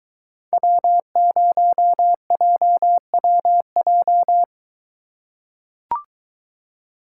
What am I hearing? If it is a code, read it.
Morse code, heard as a keyed tone, about 22 wpm, 700 Hz. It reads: W0JWJ